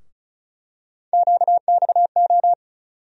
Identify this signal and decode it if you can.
Morse code, heard as a keyed tone, about 36 wpm, 700 Hz. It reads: QXO
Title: USA Call Sign Postfixes - 3 Letters 35wpm